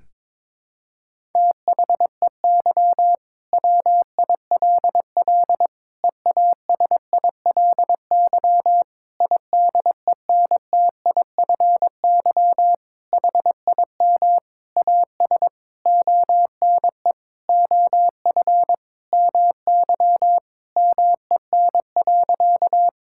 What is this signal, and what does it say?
Morse code, heard as a keyed tone, about 22 wpm, 700 Hz. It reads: THEY WILL EASILY IDENTIFY HIM AS ONE OF MY MEN.
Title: Sentences from Top 1000 Words - Review Part 6 - 22wpm